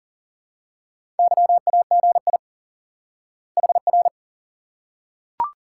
Morse code, heard as a keyed tone, about 40 wpm, 700 Hz. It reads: YAGI HR
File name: IQE Course 3 - Lesson 89 - YAGI - Two Elements - 40wpm